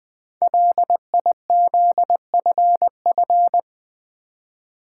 Morse code, heard as a keyed tone, about 20 wpm, 700 Hz. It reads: LIZFF
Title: Code Groups - 5 Letters - 20wpm